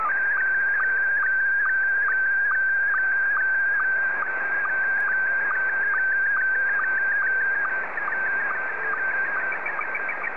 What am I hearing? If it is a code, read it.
SSTV (slow-scan television), Scottie 1.